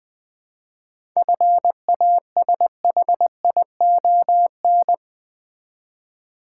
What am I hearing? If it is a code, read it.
Morse code, heard as a keyed tone, about 20 wpm, 700 Hz. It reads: FASHION
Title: 7 Letter Words - First 3 Letters Given - 20wpm